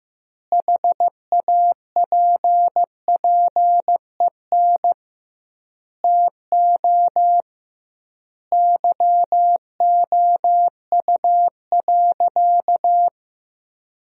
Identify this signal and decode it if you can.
Morse code, heard as a keyed tone, about 15 wpm, 700 Hz. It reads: HAPPEN TO YOU.
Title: 2x Word-Spacing - Sentences from Top 500 Words - 15wpm